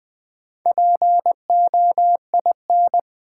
Morse code, heard as a keyed tone, about 20 wpm, 700 Hz. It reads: POIN
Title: Build A Word - 1 Letter At A Time - 5-Letter Words - 20wpm